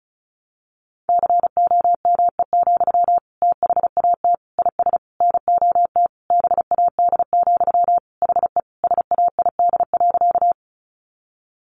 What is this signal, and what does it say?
Morse code, heard as a keyed tone, about 34 wpm, 700 Hz. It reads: COME, THAT IS NOT BAD, HE SAID.
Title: Sentences from Top 700 Words 35wpm